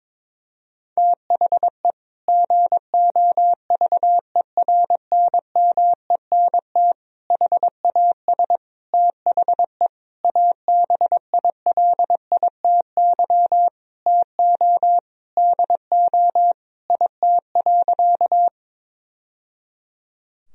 Morse code, heard as a keyed tone, about 22 wpm, 700 Hz. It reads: THE GOVERNMENT HAS THE ABILITY TO DO IT.
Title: Sentences from Top 1000 Words - Review Part 3 - 22wpm